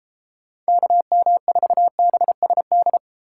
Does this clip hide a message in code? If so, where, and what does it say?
Yes: Morse code from 0.7 to 3.0 s, heard as a keyed tone, about 32 wpm, 700 Hz. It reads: KM4BSD